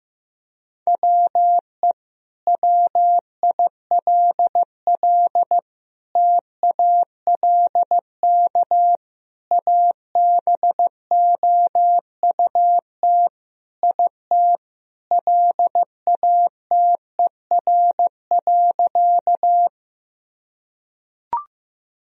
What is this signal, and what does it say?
Morse code, heard as a keyed tone, about 15 wpm, 700 Hz. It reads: WE WILL TALK ABOUT IT LATER.
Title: Sentences from Top 1000 Words - Review Part 4 - 15wpm